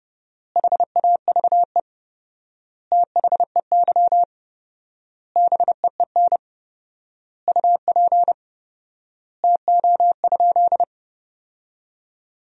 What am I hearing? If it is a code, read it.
Morse code, heard as a keyed tone, about 30 wpm, 700 Hz. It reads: HAVE THEY BEEN UP TO?
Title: 4x Word-Spacing - Sentences from Top 100 Words - 30wpm